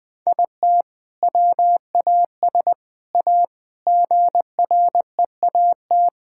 Morse code, heard as a keyed tone, about 20 wpm, 700 Hz. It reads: IT WAS A GREAT
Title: Sentences from Top 1000 Words - Review Part 6 - 20wpm